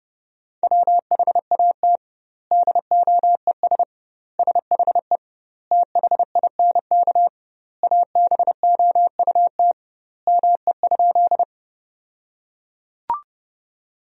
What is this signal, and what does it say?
Morse code, heard as a keyed tone, about 30 wpm, 700 Hz. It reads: WHAT DOES SHE THINK ABOUT ME?
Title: 2x Word-Spacing - Sentences from Top 200 Words - 30wpm